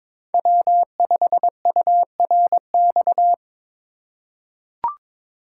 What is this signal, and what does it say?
Morse code, heard as a keyed tone, about 22 wpm, 700 Hz. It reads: W5URX
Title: USA Call Signs - 5 Characters - 4x Character Spacing - 22wpm